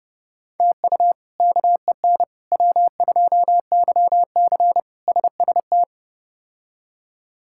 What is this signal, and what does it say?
Morse code, heard as a keyed tone, about 32 wpm, 700 Hz. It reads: TU KEN W2YC SST
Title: K1USN SST QSO Practice - 30wpm